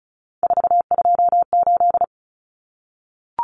Morse code, heard as a keyed tone, about 34 wpm, 700 Hz. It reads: 428
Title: Common Cut Numbers - 3-Digit 35wpm